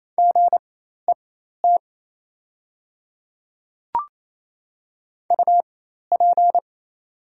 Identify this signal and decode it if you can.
Morse code, heard as a keyed tone, about 28 wpm, spaced as for 13 wpm, 700 Hz. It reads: GET UP
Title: 4x Character Spacing - Top 200 Words - 28wpm